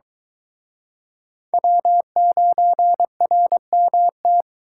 Morse code, heard as a keyed tone, about 23 wpm, 700 Hz. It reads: W9RMT